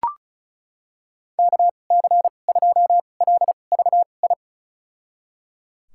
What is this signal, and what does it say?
Morse code, heard as a keyed tone, about 36 wpm, spaced as for 25 wpm, 690 Hz. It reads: KC2LVI
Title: USA Call Signs - 6 Characters - 2x Character Spacing - 35wpm